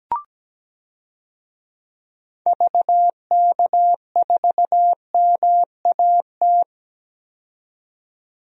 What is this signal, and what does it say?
Morse code, heard as a keyed tone, about 17 wpm, 700 Hz. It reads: VK4MAT
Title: DXCC-100 Call Signs - 6 Characters - 17wpm